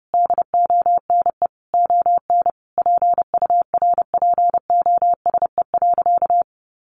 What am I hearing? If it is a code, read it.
Morse code, heard as a keyed tone, about 30 wpm, 700 Hz. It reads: DONE ON PURPOSE.
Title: Sentences from Top 800 Words 30wpm